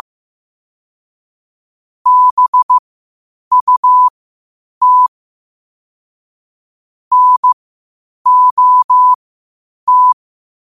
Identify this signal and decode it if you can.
Morse code, heard as a keyed tone, about 15 wpm, spaced as for 8 wpm, 1000 Hz. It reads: BUT NOT
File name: qotdfiles / 8wpm